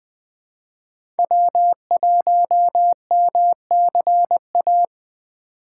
Morse code, heard as a keyed tone, about 20 wpm, 700 Hz. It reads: W1MCA
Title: USA Call Signs - 6x Character Spacing - 20wpm